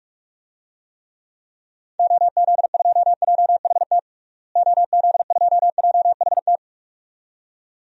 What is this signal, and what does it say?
Morse code, heard as a keyed tone, about 46 wpm, 700 Hz. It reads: OZ2JHT OZ2JHT
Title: DXCC-100 Call Signs - 6 Characters - Repeated 2x - 45wpm